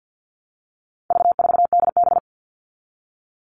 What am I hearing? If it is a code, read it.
Morse code, heard as a keyed tone, about 50 wpm, 700 Hz. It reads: V4DB
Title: All Cut Numbers - 4-Digit 50wpm